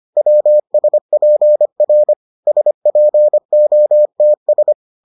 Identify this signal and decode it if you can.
Morse code, heard as a keyed tone, about 25 wpm, 600 Hz. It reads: WSPR SPOTS